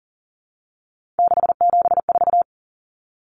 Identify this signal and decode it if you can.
Morse code, heard as a keyed tone, about 40 wpm, 700 Hz. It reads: 674
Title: Common Cut Numbers - 3-Digit 40wpm